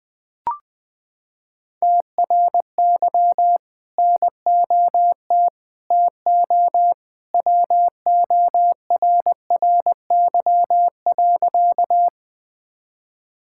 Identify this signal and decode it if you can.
Morse code, heard as a keyed tone, about 21 wpm, 700 Hz. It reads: TRY NOT TO WORRY.
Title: Sentences with 2-Word Phrases - 20wpm